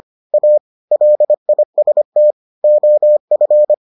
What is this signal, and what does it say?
Morse code, heard as a keyed tone, about 25 wpm, 600 Hz. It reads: A LIST OF